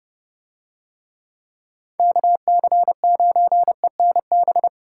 Morse code, heard as a keyed tone, about 29 wpm, 700 Hz. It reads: KC9ENB